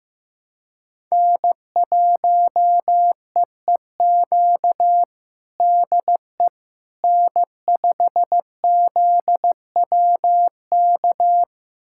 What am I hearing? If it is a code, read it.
Morse code, heard as a keyed tone, about 15 wpm, 700 Hz. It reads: N1EEQ DE N5ZWK
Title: USA Callsign DE Callsign - 5 Characters - 15wpm